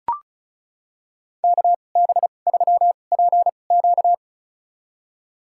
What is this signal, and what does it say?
Morse code, heard as a keyed tone, about 36 wpm, spaced as for 25 wpm, 700 Hz. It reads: KB3PQ
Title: USA Call Signs - 5 Characters - 2x Character Spacing - 35wpm